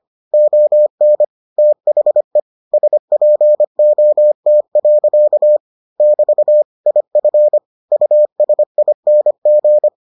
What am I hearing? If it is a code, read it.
Morse code, heard as a keyed tone, about 25 wpm, 600 Hz. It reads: ON THE SPOT. <BT> IF USING